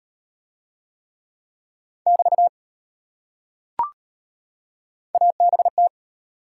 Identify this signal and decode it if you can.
Morse code, heard as a keyed tone, about 38 wpm, 700 Hz. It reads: <BT> ABT